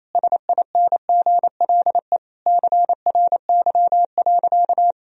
Morse code, heard as a keyed tone, about 28 wpm, 700 Hz. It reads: SINGLE CRY.
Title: Sentences from Top 1000 Words - Review Part 4 - 28wpm